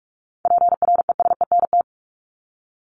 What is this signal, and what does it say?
Morse code, heard as a keyed tone, about 44 wpm, 700 Hz. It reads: PRESENT
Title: ED Suffix - Top 100 Words with Base Words 45wpm